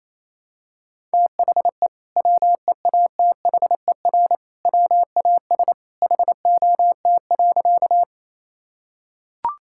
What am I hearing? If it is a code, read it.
Morse code, heard as a keyed tone, about 28 wpm, 700 Hz. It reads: THE WEATHER WAS HOT.
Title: Sentences from Top 1000 Words - Review Part 5 - 28wpm